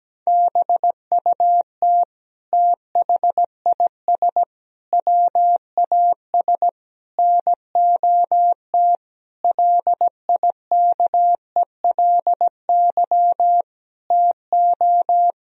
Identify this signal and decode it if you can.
Morse code, heard as a keyed tone, about 17 wpm, 700 Hz. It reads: BUT THIS WAS NOT LIKELY TO
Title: Sentences from Top 1000 Words - Review Part 4 - 17wpm